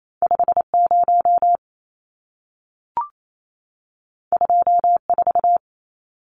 Morse code, heard as a keyed tone, about 28 wpm, 700 Hz. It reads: 50 24
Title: Numbers - 2-Digit 28wpm